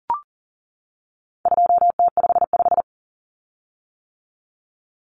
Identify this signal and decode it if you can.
Morse code, heard as a keyed tone, about 40 wpm, 700 Hz. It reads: 2T55
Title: Common Cut Numbers - 4-Digit 40wpm